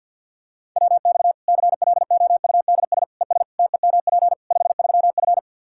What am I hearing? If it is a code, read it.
Morse code, heard as a keyed tone, about 48 wpm, 700 Hz. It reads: WX CLOUDS ES TEMP 53F
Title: Practice QSO - Exchange 2 - Info - 50wpm